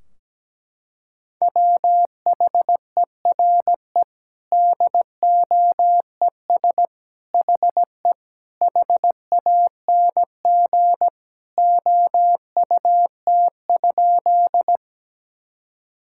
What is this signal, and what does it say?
Morse code, heard as a keyed tone, about 17 wpm, 700 Hz. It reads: WHERE DOES HE HANG OUT?
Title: Sentences from Top 1600 Words 17wpm